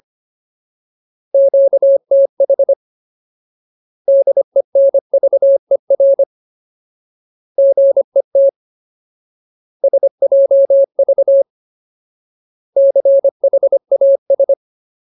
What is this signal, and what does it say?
Morse code, heard as a keyed tone, about 25 wpm, 550 Hz. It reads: QTH DENVER GET SJV CHAS
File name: Ultimate Word List / 25WPM Ultimate